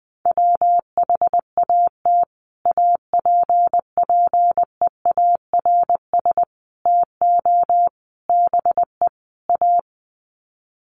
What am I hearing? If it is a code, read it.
Morse code, heard as a keyed tone, about 19 wpm, 700 Hz. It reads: WHAT APPEARS TO BE A
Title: Sets of 5 Words Encore2 20wpm